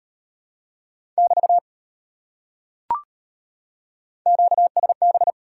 Morse code, heard as a keyed tone, about 38 wpm, 700 Hz. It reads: <BT> QSB